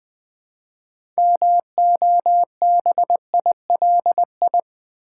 Morse code, heard as a keyed tone, about 20 wpm, 700 Hz. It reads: MOBILI
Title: Build A Word - 2 Letters At A Time - 8-Letter Words - 20wpm